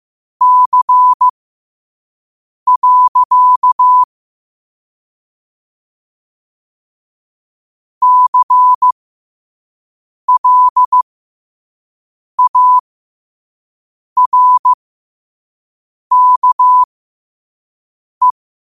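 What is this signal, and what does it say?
Morse code, heard as a keyed tone, about 15 wpm, spaced as for 5 wpm, 1000 Hz. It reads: C. CLARKE
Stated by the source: Morse at 5 wpm.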